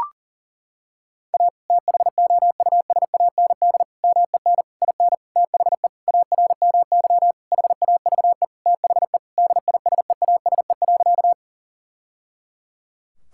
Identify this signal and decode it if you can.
Morse code, heard as a keyed tone, about 37 wpm, 700 Hz. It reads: A THOUSAND MEN IN THE ARMY HAVE THE DISEASE.
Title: Sentences from Top 1000 Words - Review Part 6 - 40wpm